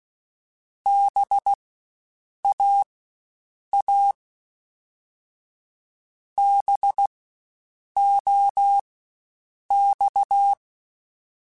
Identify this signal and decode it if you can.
Morse code, heard as a keyed tone, about 16 wpm, spaced as for 7 wpm, 800 Hz. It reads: BAA BOX
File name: Spaced Alphas / Spaced Alphas mp3s / Bs sp @15